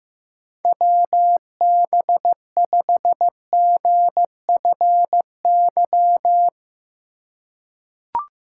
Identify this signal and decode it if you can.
Morse code, heard as a keyed tone, about 15 wpm, 700 Hz. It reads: WB5GFY
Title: USA Call Signs - 3x Character Spacing - 15wpm